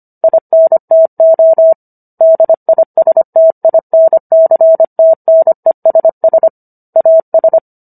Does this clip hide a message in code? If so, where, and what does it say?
Yes: Morse code from 0.2 to 7.6 s, heard as a keyed tone, about 25 wpm, 650 Hz. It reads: INTO DISTINCTNESS AS